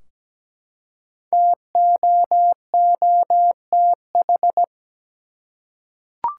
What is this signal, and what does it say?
Morse code, heard as a keyed tone, about 17 wpm, 700 Hz. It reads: TOOTH
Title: Five Letter Words - 17wpm